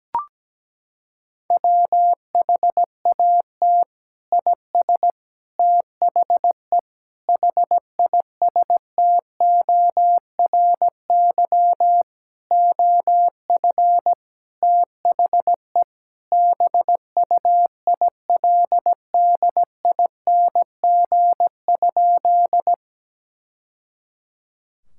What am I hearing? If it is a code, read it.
Morse code, heard as a keyed tone, about 17 wpm, 700 Hz. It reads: WHAT IS THE HISTORY OF THE BUILDING?
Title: Sentences from Top 1000 Words - Review Part 5 - 17wpm